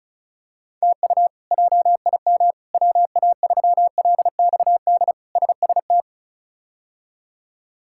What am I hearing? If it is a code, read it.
Morse code, heard as a keyed tone, about 36 wpm, 700 Hz. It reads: TU JIM WA3LXD SST
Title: K1USN SST QSO Practice - 35wpm